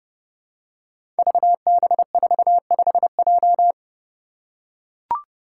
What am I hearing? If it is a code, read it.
Morse code, heard as a keyed tone, about 30 wpm, 700 Hz. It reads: VB45J
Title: Code Groups - 5 Letter:Numbers - 30wpm